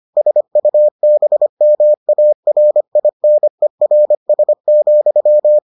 Morse code, heard as a keyed tone, about 25 wpm, 600 Hz. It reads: SUBMARINERS,